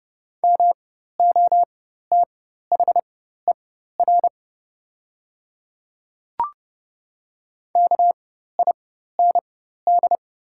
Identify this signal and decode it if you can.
Morse code, heard as a keyed tone, about 30 wpm, spaced as for 14 wpm, 700 Hz. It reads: MOTHER KIND
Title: 4x Character Spacing - Top 200 Words - 30wpm